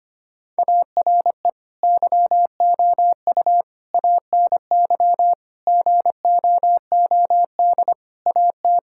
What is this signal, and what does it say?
Morse code, heard as a keyed tone, about 25 wpm, 700 Hz. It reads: ARE YOU ANY GOOD AT
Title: Sentences from Top 1000 Words - Review Part 6 - 25wpm